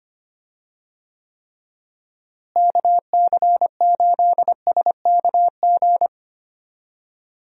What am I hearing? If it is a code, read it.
Morse code, heard as a keyed tone, about 25 wpm, 700 Hz. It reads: KC8SKG
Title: USA Call Signs - 6 Characters - Repeated 2x - 25wpm